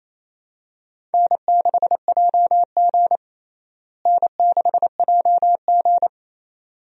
Morse code, heard as a keyed tone, about 27 wpm, 700 Hz. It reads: N6JG N6JG
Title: USA Call Signs - 4 Characters - Repeated 2x - 28wpm